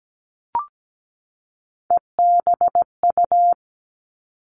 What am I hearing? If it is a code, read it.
Morse code, heard as a keyed tone, about 17 wpm, 700 Hz. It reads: EBU
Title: Three Challenging Letter-Number Combinations 17wpm